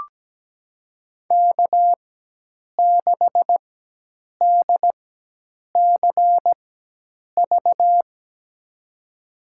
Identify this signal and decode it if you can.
Morse code, heard as a keyed tone, about 17 wpm, spaced as for 8 wpm, 700 Hz. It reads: K6DCV
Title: USA Call Signs - 5 Characters - 4x Character Spacing - 17wpm